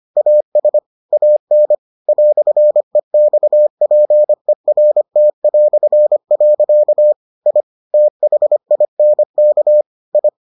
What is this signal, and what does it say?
Morse code, heard as a keyed tone, about 25 wpm, 600 Hz. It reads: AS AN "EXPERT". I THINK I